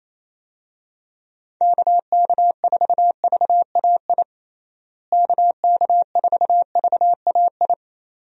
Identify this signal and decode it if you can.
Morse code, heard as a keyed tone, about 27 wpm, 700 Hz. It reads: KK4VAI KK4VAI
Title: USA Call Signs - 6 Characters - Repeated 2x - 28wpm